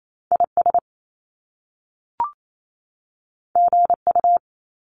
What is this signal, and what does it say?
Morse code, heard as a keyed tone, about 28 wpm, 700 Hz. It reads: IS GU
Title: Top Two Letter Combinations in Words 28wpm